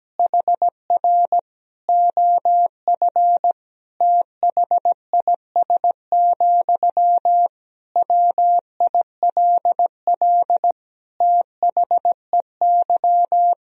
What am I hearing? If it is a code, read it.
Morse code, heard as a keyed tone, about 17 wpm, 700 Hz. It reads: HR OF THIS, WILL THEY
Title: Sentences from Top 400 Words and Abbreviations 17wpm